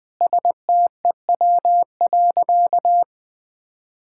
Morse code, heard as a keyed tone, about 21 wpm, 700 Hz. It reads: STEW.
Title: Easy Sentences with Hints - 20wpm